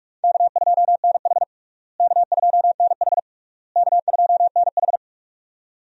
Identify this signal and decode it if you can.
Morse code, heard as a keyed tone, about 42 wpm, 700 Hz. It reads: K2NH K2NH K2NH
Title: USA Call Signs - 4 Characters - Repeated 3x - 45wpm